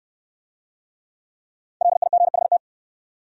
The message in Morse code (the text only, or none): LIGHT